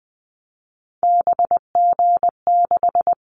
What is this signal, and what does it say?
Morse code, heard as a keyed tone, about 20 wpm, 700 Hz. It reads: BG6
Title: Callsign Prefixes for Top 100 DXCC - 20wpm